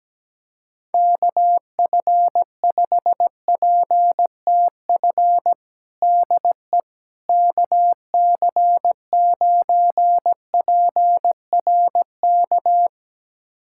Morse code, heard as a keyed tone, about 17 wpm, 700 Hz. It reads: KF5PTF DE KC9PRK
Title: USA Callsign DE Callsign - 6 Characters - 17wpm